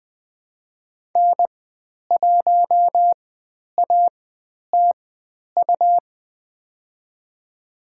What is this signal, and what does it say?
Morse code, heard as a keyed tone, about 20 wpm, spaced as for 10 wpm, 700 Hz. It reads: N1ATU
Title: 109 Lesson 18g - Letter m - Call Signs 20wpm